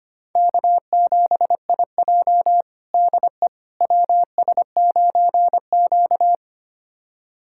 Morse code, heard as a keyed tone, about 25 wpm, 700 Hz. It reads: K7IJ DE WS9Q
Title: USA Callsign DE Callsign - 4 Characters - 25wpm